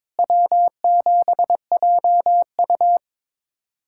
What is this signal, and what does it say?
Morse code, heard as a keyed tone, about 22 wpm, 700 Hz. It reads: W7JU
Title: USA Callsign DE Callsign 22wpm